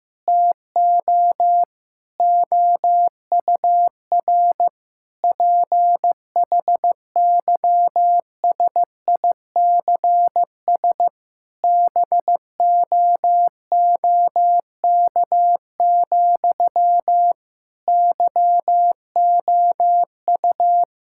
Morse code, heard as a keyed tone, about 15 wpm, 700 Hz. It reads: TO OUR PHYSICS BOOK, YOU
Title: Sentences from Top 1000 Words - Review Part 5 - 15wpm